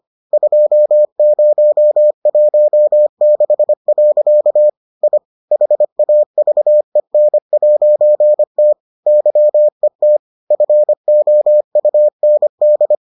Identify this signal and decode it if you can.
Morse code, heard as a keyed tone, about 25 wpm, 600 Hz. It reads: 2016. I HAVEN'T YET FOUND